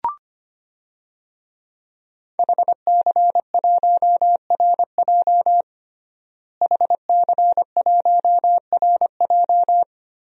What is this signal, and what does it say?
Morse code, heard as a keyed tone, about 25 wpm, 700 Hz. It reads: HC1RJ HC1RJ
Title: DXCC-100 Call Signs - 5 Characters - Repeated 3x - 25wpm